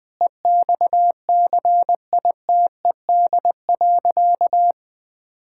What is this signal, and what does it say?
Morse code, heard as a keyed tone, about 21 wpm, 700 Hz. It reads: EXCITED.
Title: Sentences with 6-Word Phrases - 20wpm